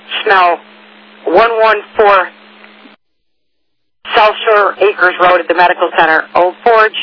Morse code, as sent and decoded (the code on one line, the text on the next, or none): none